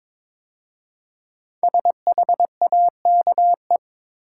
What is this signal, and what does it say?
Morse code, heard as a keyed tone, about 22 wpm, 690 Hz. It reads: SHAKE
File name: 5 Letter Words - First 3 Letters Given - 22wpm